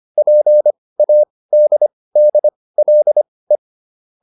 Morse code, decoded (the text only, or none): PADDLE